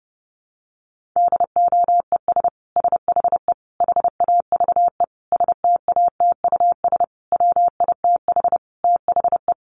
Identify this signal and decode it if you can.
Morse code, heard as a keyed tone, about 30 wpm, 700 Hz. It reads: DOES SHE HAVE STATUS WITH THE
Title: Sentences from Top 1400 Words 30wpm